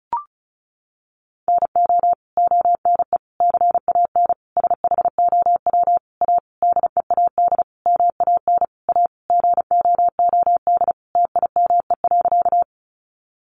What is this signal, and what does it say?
Morse code, heard as a keyed tone, about 34 wpm, 700 Hz. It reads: NO ONE CAN SHOW A DEAD MAN A GOOD TIME.
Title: Sentences from Top 1500 Words 35wpm